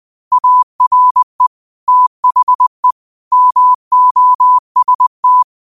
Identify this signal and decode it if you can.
Morse code, heard as a keyed tone, about 20 wpm, 1000 Hz. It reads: ARE THE MOST
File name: 20wpm